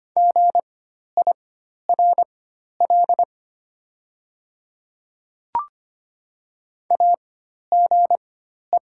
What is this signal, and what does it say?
Morse code, heard as a keyed tone, about 25 wpm, spaced as for 12 wpm, 700 Hz. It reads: GIRL AGE
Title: 4x Character Spacing - Top 500 Words - 25wpm